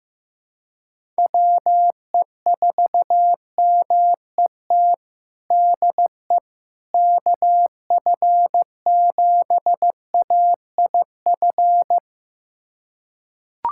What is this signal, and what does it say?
Morse code, heard as a keyed tone, about 16 wpm, 700 Hz. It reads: WE4MET DE KF7AIF
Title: USA Callsign DE Callsign - 6 Characters - 15wpm